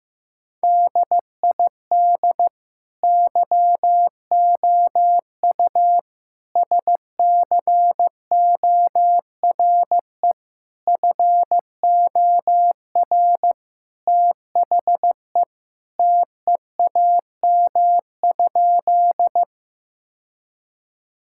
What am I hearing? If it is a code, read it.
Morse code, heard as a keyed tone, about 15 wpm, 700 Hz. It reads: DID YOU SCORE FOR THE TEAM?
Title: Sentences from Top 1600 Words 15wpm